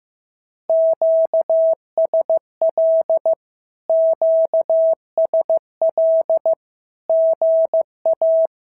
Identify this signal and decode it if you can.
Morse code, heard as a keyed tone, about 15 wpm, 650 Hz. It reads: QSL QSL GA
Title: SOTA QSO Practice 15wpm